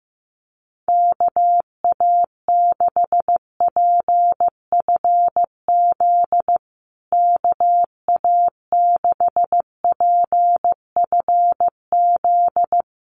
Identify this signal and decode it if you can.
Morse code, heard as a keyed tone, about 15 wpm, 700 Hz. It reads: KA6PFZ KA6PFZ
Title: USA Call Signs Repeated 2x 15wpm